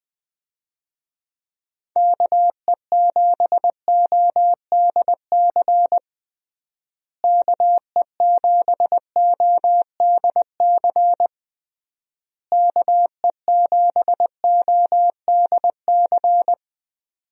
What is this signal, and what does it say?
Morse code, heard as a keyed tone, about 20 wpm, 700 Hz. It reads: KE7ODC KE7ODC KE7ODC
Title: USA Call Signs - 6 Characters - Repeated 3x - 20wpm